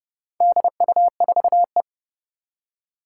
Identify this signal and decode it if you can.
Morse code, heard as a keyed tone, about 30 wpm, 700 Hz. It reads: DU4E
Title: Four Challenging Letter-Number Combinations 30wpm